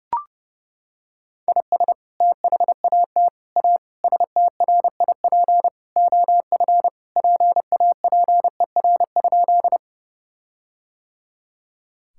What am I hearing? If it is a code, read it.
Morse code, heard as a keyed tone, about 30 wpm, 700 Hz. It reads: IS THAT A STRIP OF PAPER?
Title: Sentences from Top 1600 Words 30wpm